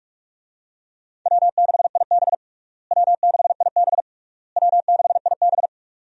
Morse code, heard as a keyed tone, about 41 wpm, 700 Hz. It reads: W6IB W6IB W6IB
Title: USA Call Signs - 4 Characters - Repeated 3x - 45wpm